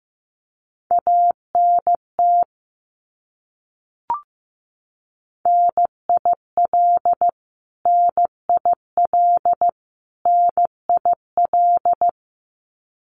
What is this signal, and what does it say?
Morse code, heard as a keyed tone, about 14 wpm, 700 Hz. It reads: ANT NIL NIL NIL
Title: Abbreviations Repeated 3x - 15wpm